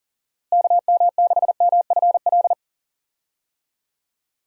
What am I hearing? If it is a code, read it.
Morse code, heard as a keyed tone, about 37 wpm, 700 Hz. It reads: KM6MFL
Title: USA Call Signs - 6 Characters - Repeated 2x - 40wpm